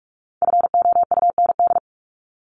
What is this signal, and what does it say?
Morse code, heard as a keyed tone, about 44 wpm, 700 Hz. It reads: FOUND
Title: Top 300 Words 45wpm